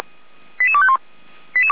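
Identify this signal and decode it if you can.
Five-tone selective call (ZVEI), heard as quick runs of five tones, tones of about 70 ms. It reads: 88251
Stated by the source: Five-tone ZVEI selective call received on SDR.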